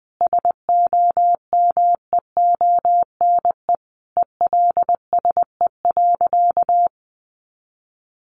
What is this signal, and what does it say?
Morse code, heard as a keyed tone, about 20 wpm, 700 Hz. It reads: SOMEONE ELSE.
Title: Sentences from Top 800 Words 20wpm